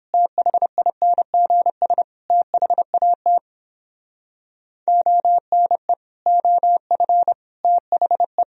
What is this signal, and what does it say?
Morse code, heard as a keyed tone, about 31 wpm, 700 Hz. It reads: THINGS THAT ONE OF THE